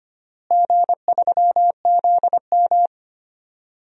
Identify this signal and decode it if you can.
Morse code, heard as a keyed tone, about 25 wpm, 700 Hz. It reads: G3ZM
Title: DXCC-100 Call Signs - 4 Characters - Repeated 3x - 25wpm